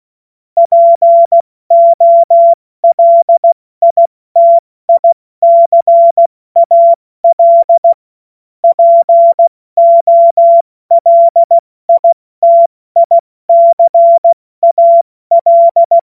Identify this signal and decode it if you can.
Morse code, heard as a keyed tone, about 16 wpm, spaced as for 14 wpm, 680 Hz. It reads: POLITICAL POLITICAL
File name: Morsekurs / Witze / Witz 27. September 2022 70 F80